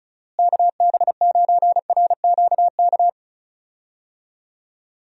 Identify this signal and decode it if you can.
Morse code, heard as a keyed tone, about 34 wpm, 700 Hz. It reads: KB9RQK
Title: USA Call Signs - 6 Characters - Repeated 2x - 35wpm